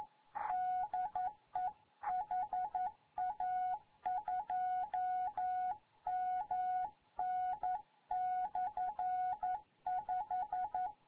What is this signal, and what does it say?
Morse code, heard as a keyed tone, about 11 wpm, 740 Hz. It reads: DE HA2MN/5
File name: GYAKORLAT